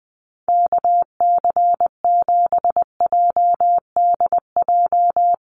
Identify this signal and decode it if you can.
Morse code, heard as a keyed tone, about 20 wpm, 700 Hz. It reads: KC7JDJ
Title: USA Call Signs Repeated 2x 20wpm